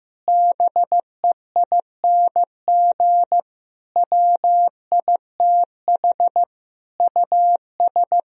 Morse code, heard as a keyed tone, about 15 wpm, 700 Hz. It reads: BEING WITH US
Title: Speed-Racing - Sets of 5 Words 15wpm